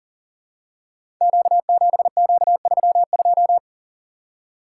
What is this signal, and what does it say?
Morse code, heard as a keyed tone, about 40 wpm, 700 Hz. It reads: Q7Q32
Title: Code Groups - 5 Letter:Numbers - 40wpm